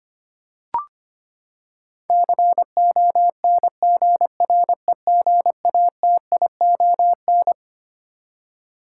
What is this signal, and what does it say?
Morse code, heard as a keyed tone, about 25 wpm, 700 Hz. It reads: CONGREGATION
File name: Twelve Letter Words - 25wpm